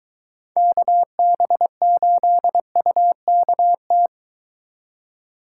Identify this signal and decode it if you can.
Morse code, heard as a keyed tone, about 22 wpm, 700 Hz. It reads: KB8UKT